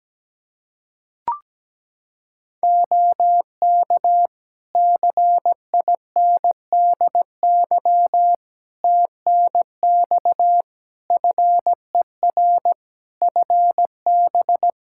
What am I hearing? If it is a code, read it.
Morse code, heard as a keyed tone, about 17 wpm, 700 Hz. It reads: OK CINDY TNX FER FB
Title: K1USN SST QSO Practice 17wpm